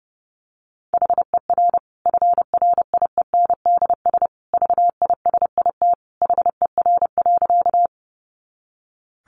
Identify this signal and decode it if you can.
Morse code, heard as a keyed tone, about 29 wpm, 700 Hz. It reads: HER FRIENDS VISIT HER.
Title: Easy Phrases 30wpm